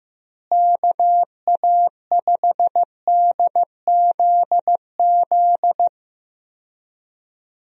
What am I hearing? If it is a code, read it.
Morse code, heard as a keyed tone, about 15 wpm, 700 Hz. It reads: KA5DZZ
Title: USA Call Signs - 6 Characters - Repeated 2x - 15wpm